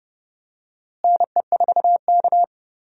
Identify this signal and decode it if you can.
Morse code, heard as a keyed tone, about 31 wpm, 700 Hz. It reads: NE4K
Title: USA Call Signs - 4 Characters - Repeated 3x - 30wpm